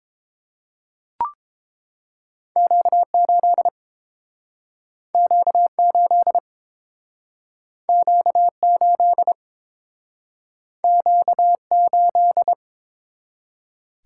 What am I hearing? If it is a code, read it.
Morse code, heard as a keyed tone, about 30 wpm, 700 Hz. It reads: Q8 Q8 Q8 Q8